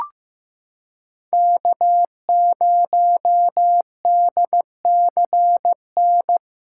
Morse code, heard as a keyed tone, about 15 wpm, 700 Hz. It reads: K0DCN
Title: USA Callsign DE Callsign - 5 Characters - 15wpm